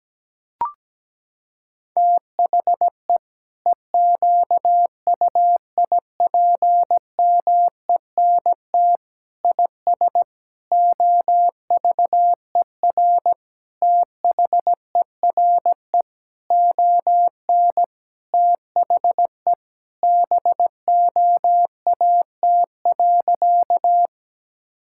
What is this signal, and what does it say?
Morse code, heard as a keyed tone, about 17 wpm, 700 Hz. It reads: THE EQUIPMENT IS OVER THERE ON THE BOAT.
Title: Sentences from Top 1000 Words - Review Part 5 - 17wpm